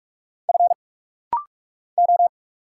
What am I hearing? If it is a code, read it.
Morse code, heard as a keyed tone, about 46 wpm, 700 Hz. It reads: F O
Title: Single Letters - Rapid-Fire 45wpm